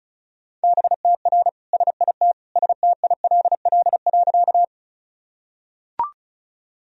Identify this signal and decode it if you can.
Morse code, heard as a keyed tone, about 36 wpm, 700 Hz. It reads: BTR SIT STILL.
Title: Sentences from Top 500 Words and Abbreviations 35wpm